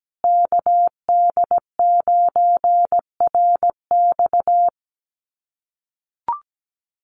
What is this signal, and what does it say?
Morse code, heard as a keyed tone, about 17 wpm, 700 Hz. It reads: KD9RX
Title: USA Call Signs Repeated 3x 17wpm